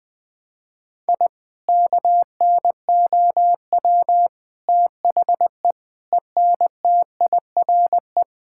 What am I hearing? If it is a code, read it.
Morse code, heard as a keyed tone, about 20 wpm, 700 Hz. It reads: I KNOW THE ENTIRE
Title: Sentences from Top 1000 Words - Review Part 5 - 20wpm